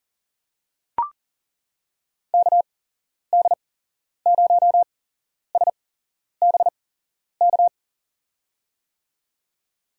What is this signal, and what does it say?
Morse code, heard as a keyed tone, about 40 wpm, spaced as for 11 wpm, 700 Hz. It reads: KD0SBK
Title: USA Call Signs - 8x Character Spacing - 40wpm